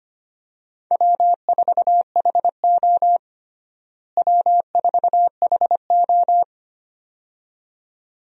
Morse code, heard as a keyed tone, about 25 wpm, 700 Hz. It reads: W4HO W4HO
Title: USA Call Signs - 4 Characters - Repeated 2x - 25wpm